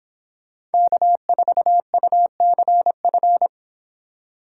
Morse code, heard as a keyed tone, about 29 wpm, 700 Hz. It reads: K4UCF